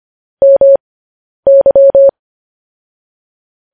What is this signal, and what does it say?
Morse code, heard as a keyed tone, about 25 wpm, spaced as for 10 wpm, 550 Hz. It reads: MY
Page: news headlines in morse code at 10 wpm